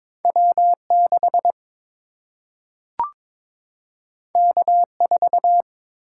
Morse code, heard as a keyed tone, about 22 wpm, 700 Hz. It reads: W 6 K 4
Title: Single-Letter Single-Number 22wpm